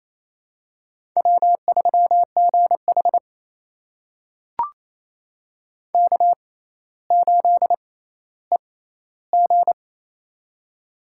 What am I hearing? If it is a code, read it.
Morse code, heard as a keyed tone, about 28 wpm, spaced as for 14 wpm, 700 Hz. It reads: W3GH K8EG
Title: USA Call Signs - 4 Characters - 6x Character Spacing - 28wpm